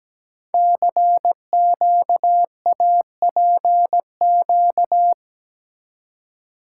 Morse code, heard as a keyed tone, about 17 wpm, 700 Hz. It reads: CQAPQ
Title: Code Groups - 5 Letters - 17wpm